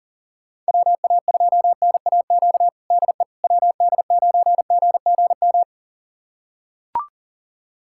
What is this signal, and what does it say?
Morse code, heard as a keyed tone, about 40 wpm, 700 Hz. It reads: WA2NAQ DE WD9GGM
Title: USA Callsign DE Callsign - 6 Characters - 40wpm